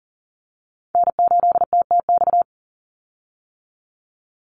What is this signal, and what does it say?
Morse code, heard as a keyed tone, about 40 wpm, 700 Hz. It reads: N8TTX
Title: USA Call Signs 40wpm